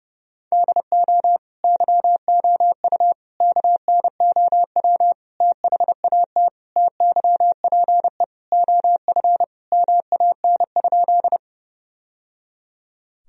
Morse code, heard as a keyed tone, about 32 wpm, 700 Hz. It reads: DO YOU KNOW THAT TYPE OF MAN?
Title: Sentences from Top 1000 Words - Review Part 3 - 30wpm